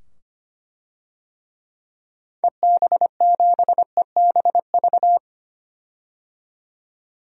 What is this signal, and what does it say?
Morse code, heard as a keyed tone, about 25 wpm, 700 Hz. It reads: EB7EBV
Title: DXCC-100 Call Signs - 6 Characters - 25wpm